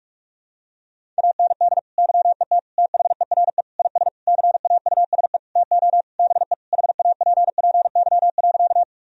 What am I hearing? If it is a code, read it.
Morse code, heard as a keyed tone, about 41 wpm, 700 Hz. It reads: AND YET THERE IS CAUSE TO BE HAPPY.
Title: Sentences from Top 1000 Words - Review Part 1 - 45wpm